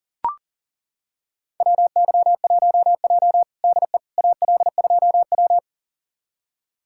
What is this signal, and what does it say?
Morse code, heard as a keyed tone, about 37 wpm, 700 Hz. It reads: WY1J DE AL2W
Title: USA Callsign DE Callsign - 4 Characters - 40wpm